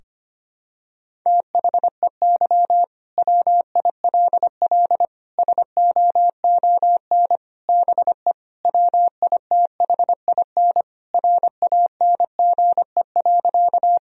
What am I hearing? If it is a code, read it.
Morse code, heard as a keyed tone, about 25 wpm, 700 Hz. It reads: THEY WILL SOON BE WITHIN RANGE.
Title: Sentences from Top 1000 Words - Review Part 4 - 25wpm